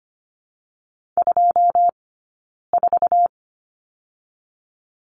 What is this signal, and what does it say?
Morse code, heard as a keyed tone, about 25 wpm, 700 Hz. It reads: 2 4
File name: Lesson 15d - Number 5 - Two Characters 25wpm